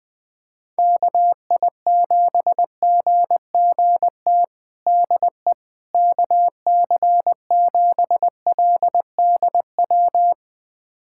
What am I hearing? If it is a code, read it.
Morse code, heard as a keyed tone, about 21 wpm, 700 Hz. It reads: KI7GGT DE KC7LDW
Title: USA Callsign DE Callsign - 6 Characters - 20wpm